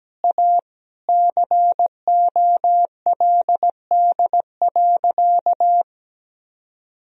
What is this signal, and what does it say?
Morse code, heard as a keyed tone, about 17 wpm, 700 Hz. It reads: A COLD.
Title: Sentences from Top 1000 Words - Review Part 3 - 17wpm